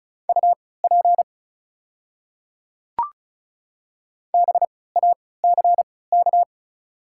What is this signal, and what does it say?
Morse code, heard as a keyed tone, about 36 wpm, spaced as for 20 wpm, 690 Hz. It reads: UP BACK
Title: 3x Character Spacing - Top 200 Words - 35wpm